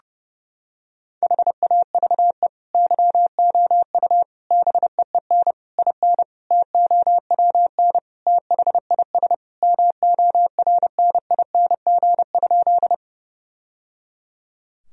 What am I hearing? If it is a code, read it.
Morse code, heard as a keyed tone, about 30 wpm, 700 Hz. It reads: HAVE YOU BEEN IN TOWN THIS MORNING?
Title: Sentences from Top 1000 Words - Review Part 2 - 30wpm